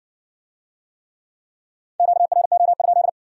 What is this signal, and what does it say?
Morse code, heard as a keyed tone, about 56 wpm, 700 Hz. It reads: QRO?